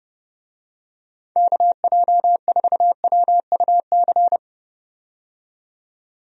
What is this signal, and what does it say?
Morse code, heard as a keyed tone, about 30 wpm, 700 Hz. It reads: KJ4WUC
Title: USA Call Signs - 6 Characters - 30wpm